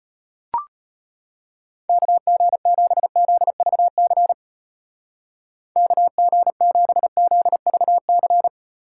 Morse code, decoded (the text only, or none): KG7ZVC KG7ZVC